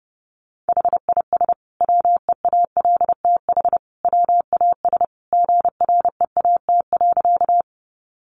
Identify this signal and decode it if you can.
Morse code, heard as a keyed tone, about 30 wpm, 700 Hz. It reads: HIS WEALTH WAS GREAT.
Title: Sentences from Top 1200 Words 30wpm